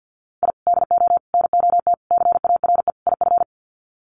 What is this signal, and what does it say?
Morse code, heard as a keyed tone, about 50 wpm, 700 Hz. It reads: I DO NOT CARE IF
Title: Sets of 5 Words 50wpm